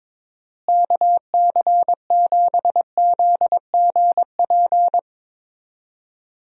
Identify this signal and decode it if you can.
Morse code, heard as a keyed tone, about 22 wpm, 700 Hz. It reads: KC7ZGP
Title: USA Call Signs - 6 Characters - Repeated 2x - 22wpm